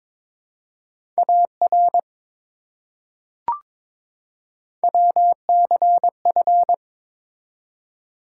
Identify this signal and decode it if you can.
Morse code, heard as a keyed tone, about 22 wpm, 700 Hz. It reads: AR WCF
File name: ARRL Sections v2 - 22wpm